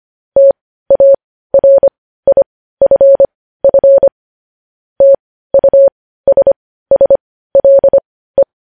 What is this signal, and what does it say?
Morse code, heard as a keyed tone, about 25 wpm, spaced as for 15 wpm, 550 Hz. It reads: TARIFF TUSSLE
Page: news headlines in morse code at 15 wpm